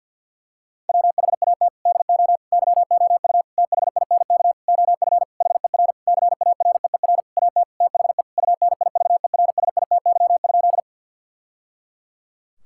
Morse code, heard as a keyed tone, about 50 wpm, 700 Hz. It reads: WHAT DO YOU THINK OF HER CAREER AT THE UNIVERSITY?
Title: Sentences from Top 1000 Words - Review Part 6 - 50wpm